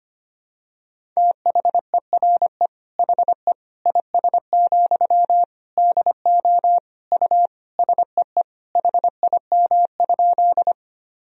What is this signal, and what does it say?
Morse code, heard as a keyed tone, about 25 wpm, 700 Hz. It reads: THERE HE IS, DO U SEE HIM?
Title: Sentences from Top 100 Words and Abbreviations 25wpm